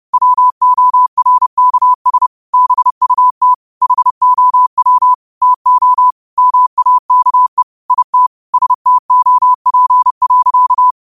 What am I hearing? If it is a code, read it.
Morse code, heard as a keyed tone, about 30 wpm, 1000 Hz. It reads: WORKS BUT HOW TO MAKE IT STOP.